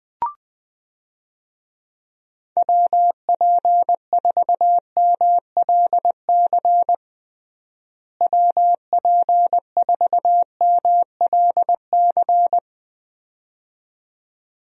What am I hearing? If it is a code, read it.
Morse code, heard as a keyed tone, about 20 wpm, 700 Hz. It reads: WP4MLC WP4MLC
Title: USA Call Signs - 6 Characters - Repeated 2x - 20wpm